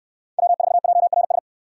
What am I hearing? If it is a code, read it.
Morse code, heard as a keyed tone, about 69 wpm, 700 Hz. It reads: W4JRS